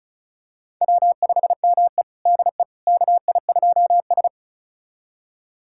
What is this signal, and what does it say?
Morse code, heard as a keyed tone, about 34 wpm, 700 Hz. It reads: W5ME DE KI2S
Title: USA Callsign DE Callsign - 4 Characters - 35wpm